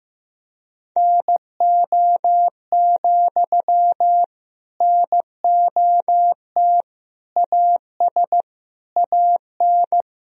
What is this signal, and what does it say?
Morse code, heard as a keyed tone, about 15 wpm, 700 Hz. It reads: NO, NOT AS AN
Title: Sentences from Top 1000 Words - Review Part 4 - 15wpm